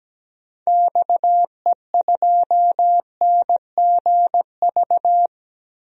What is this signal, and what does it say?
Morse code, heard as a keyed tone, about 17 wpm, 700 Hz. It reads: XE2NGV
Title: DXCC-100 Call Signs - 6 Characters - Repeated 3x - 17wpm